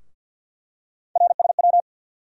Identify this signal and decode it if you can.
Morse code, heard as a keyed tone, about 51 wpm, 700 Hz. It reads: RSW
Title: USA Call Sign Postfixes - 3 Letters 50wpm